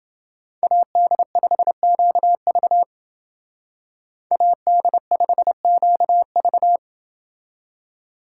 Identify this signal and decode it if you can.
Morse code, heard as a keyed tone, about 30 wpm, 700 Hz. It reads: AD5QV AD5QV